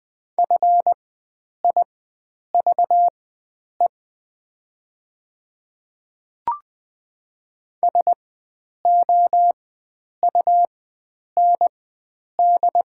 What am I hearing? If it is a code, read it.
Morse code, heard as a keyed tone, about 21 wpm, spaced as for 9 wpm, 700 Hz. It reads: FIVE SOUND
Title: 4x Character Spacing - Top 500 Words - 20wpm